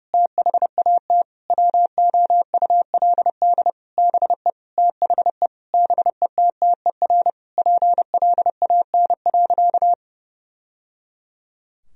Morse code, heard as a keyed tone, about 32 wpm, 700 Hz. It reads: THAT WOULD BE THE BETTER PLAN.
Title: Sentences from Top 1000 Words - Review Part 2 - 30wpm